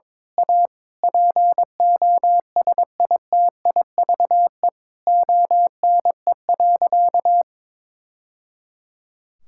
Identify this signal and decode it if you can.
Morse code, heard as a keyed tone, about 22 wpm, 700 Hz. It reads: A POSITIVE ONE.
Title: Sentences from Top 1000 Words - Review Part 6 - 22wpm